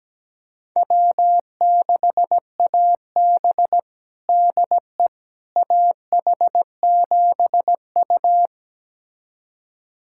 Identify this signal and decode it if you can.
Morse code, heard as a keyed tone, about 17 wpm, 700 Hz. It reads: W6AB DE AH7U
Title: USA Callsign DE Callsign - 4 Characters - 17wpm